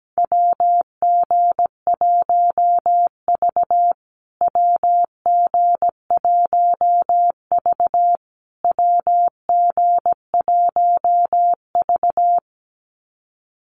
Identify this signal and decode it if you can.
Morse code, heard as a keyed tone, about 17 wpm, 700 Hz. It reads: WG1V WG1V WG1V
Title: USA Call Signs Repeated 3x 17wpm